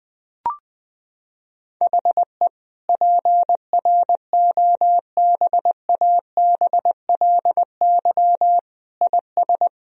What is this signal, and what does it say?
Morse code, heard as a keyed tone, about 21 wpm, 700 Hz. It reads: HE PROBABLY IS
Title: Sentences from Top 1000 Words - Review Part 4 - 20wpm